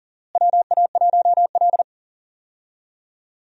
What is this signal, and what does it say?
Morse code, heard as a keyed tone, about 39 wpm, 700 Hz. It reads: WA1L
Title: USA Call Signs - 4 Characters - Repeated 3x - 40wpm